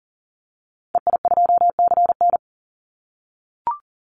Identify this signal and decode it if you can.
Morse code, heard as a keyed tone, about 40 wpm, 700 Hz. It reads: EI2CN
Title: Call Signs - 40wpm